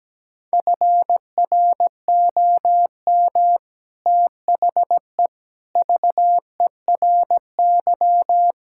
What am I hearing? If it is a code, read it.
Morse code, heard as a keyed tone, about 17 wpm, 700 Hz. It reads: FROM THE VERY
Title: Sentences from Top 1600 Words 17wpm